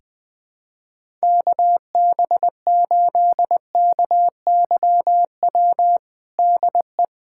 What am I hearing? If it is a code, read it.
Morse code, heard as a keyed tone, about 20 wpm, 700 Hz. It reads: KB8KYW DE
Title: USA Callsign DE Callsign 20wpm